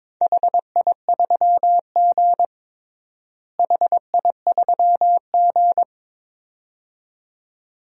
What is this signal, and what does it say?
Morse code, heard as a keyed tone, about 22 wpm, 700 Hz. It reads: HI3G HI3G
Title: DXCC-100 Call Signs - 4 Characters - Repeated 2x - 22wpm